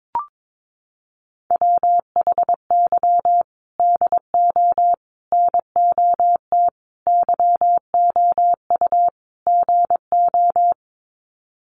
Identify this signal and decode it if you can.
Morse code, heard as a keyed tone, about 22 wpm, 700 Hz. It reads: WHY DO NOT YOU GO
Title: Sets of 5 Words Encore 22wpm